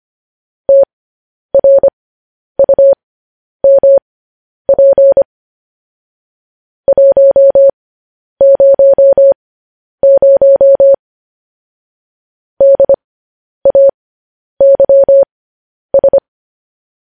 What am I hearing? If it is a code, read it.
Morse code, heard as a keyed tone, about 25 wpm, spaced as for 10 wpm, 550 Hz. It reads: TRUMP 100 DAYS
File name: podcast / News headlines-1x-25x10